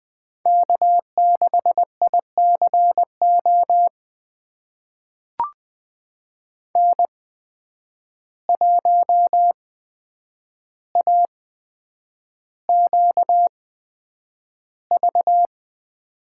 Morse code, heard as a keyed tone, about 20 wpm, spaced as for 9 wpm, 700 Hz. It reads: K6ICO N1AQV
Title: USA Call Signs - 5 Characters - 8x Character Spacing - 20wpm